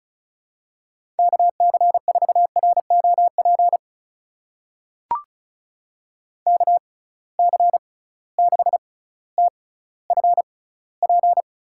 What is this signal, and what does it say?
Morse code, heard as a keyed tone, about 36 wpm, spaced as for 18 wpm, 700 Hz. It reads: KC4ROP KC6TFP
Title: USA Call Signs - 6 Characters - 6x Character Spacing - 35wpm